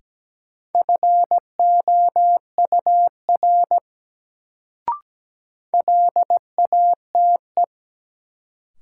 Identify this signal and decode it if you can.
Morse code, heard as a keyed tone, about 17 wpm, 700 Hz. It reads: FOUR LATE
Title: Sentences from Top 300 Words - New Word Drill - 17wpm